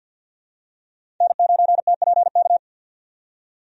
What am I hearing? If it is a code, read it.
Morse code, heard as a keyed tone, about 51 wpm, 700 Hz. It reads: N9TPK
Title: USA Call Signs - 5 Characters - 50wpm